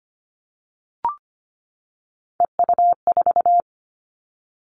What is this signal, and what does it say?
Morse code, heard as a keyed tone, about 25 wpm, 700 Hz. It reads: EU4
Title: Three Challenging Letter-Number Combinations 25wpm